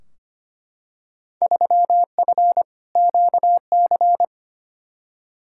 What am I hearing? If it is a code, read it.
Morse code, heard as a keyed tone, about 25 wpm, 700 Hz. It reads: 3F QC
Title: Field Day Exchanges - 25wpm